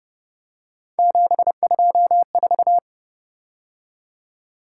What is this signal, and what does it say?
Morse code, heard as a keyed tone, about 30 wpm, 700 Hz. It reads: 724
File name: Speed-Racing - Three Letter-Number Combinations - 25wpm